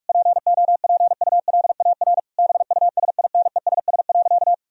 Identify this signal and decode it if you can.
Morse code, heard as a keyed tone, about 46 wpm, 700 Hz. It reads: POPULAR BUSINESS.
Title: Sentences from Top 1000 Words - Review Part 3 - 45wpm